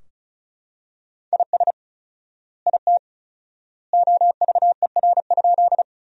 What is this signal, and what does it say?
Morse code, heard as a keyed tone, about 34 wpm, 700 Hz. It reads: IS IT OVER?
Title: 4x Word-Spacing - Sentences from Top 100 Words - 35wpm